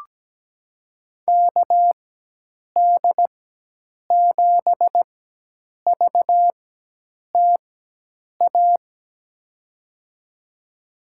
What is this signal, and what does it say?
Morse code, heard as a keyed tone, about 17 wpm, spaced as for 8 wpm, 700 Hz. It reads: KD7VTA
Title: USA Call Signs - 4x Character Spacing - 17wpm